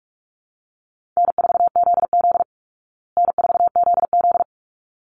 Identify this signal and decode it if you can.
Morse code, heard as a keyed tone, about 44 wpm, 700 Hz. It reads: N4ZZ N4ZZ
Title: Call Signs Repeated 3x - 45wpm